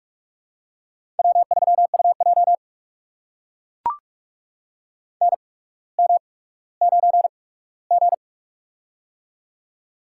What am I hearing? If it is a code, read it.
Morse code, heard as a keyed tone, about 44 wpm, spaced as for 19 wpm, 690 Hz. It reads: W3UJ NM9G
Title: USA Call Signs - 4 Characters - 8x Character Spacing - 45wpm